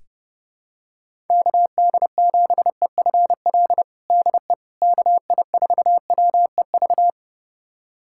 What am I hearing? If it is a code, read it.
Morse code, heard as a keyed tone, about 30 wpm, 700 Hz. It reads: KD7EFL DE KI4WEV
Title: USA Callsign DE Callsign - 6 Characters - 30wpm